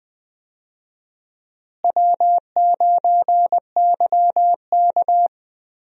Morse code, heard as a keyed tone, about 20 wpm, 700 Hz. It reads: W9YK
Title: USA Call Signs - 4 Characters - Repeated 3x - 20wpm